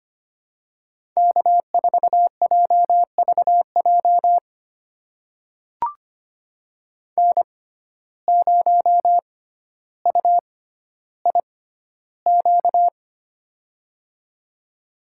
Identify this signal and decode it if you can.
Morse code, heard as a keyed tone, about 25 wpm, spaced as for 13 wpm, 700 Hz. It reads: K4JVJ N0UIQ
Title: USA Call Signs - 5 Characters - 6x Character Spacing - 25wpm